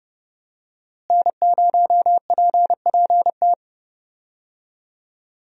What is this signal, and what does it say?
Morse code, heard as a keyed tone, about 30 wpm, 700 Hz. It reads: N0PPT
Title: USA Call Signs - 5 Characters - Repeated 3x - 30wpm